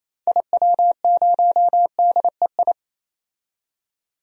Morse code, heard as a keyed tone, about 28 wpm, 700 Hz. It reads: IW0DEI
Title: DXCC-100 Call Signs - 6 Characters - 28wpm